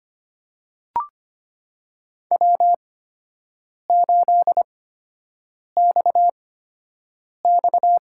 Morse code, heard as a keyed tone, about 25 wpm, spaced as for 7 wpm, 700 Hz. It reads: W8XX
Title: USA Call Signs - 8x Character Spacing - 25wpm